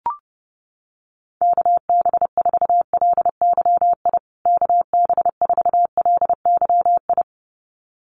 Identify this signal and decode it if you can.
Morse code, heard as a keyed tone, about 30 wpm, 700 Hz. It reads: KB4LYI KB4LYI
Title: USA Call Signs Repeated 2x 30wpm